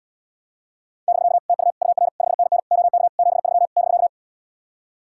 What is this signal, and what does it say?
Morse code, heard as a keyed tone, about 79 wpm, 700 Hz. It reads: <BK> TU UR 5NN OK 73 <BK>